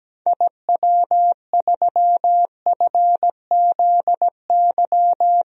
Morse code, heard as a keyed tone, about 17 wpm, 700 Hz. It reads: IW3FZY
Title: DXCC-100 Call Signs - 6 Characters - Repeated 3x - 17wpm